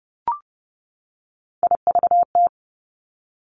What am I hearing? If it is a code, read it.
Morse code, heard as a keyed tone, about 30 wpm, 700 Hz. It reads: IVT
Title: Three Challenging Letter-Number Combinations 30wpm